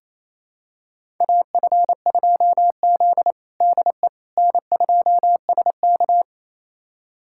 Morse code, heard as a keyed tone, about 28 wpm, 700 Hz. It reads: AF2Z DE N2SK
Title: USA Callsign DE Callsign - 4 Characters - 28wpm